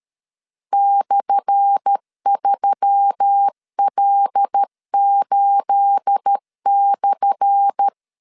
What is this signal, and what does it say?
Morse code, heard as a keyed tone, about 13 wpm, 800 Hz. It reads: /3L8/
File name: LEZIONI / 10 SIGNS / 10A SLASH 12wpm 2314